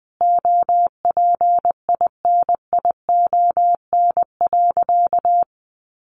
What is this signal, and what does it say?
Morse code, heard as a keyed tone, about 20 wpm, 700 Hz. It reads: OPINION.
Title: Sentences from Top 1400 Words 20wpm